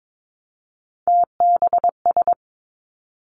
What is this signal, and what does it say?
Morse code, heard as a keyed tone, about 22 wpm, 700 Hz. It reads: TBS
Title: Three Challenging Letter-Number Combinations 22wpm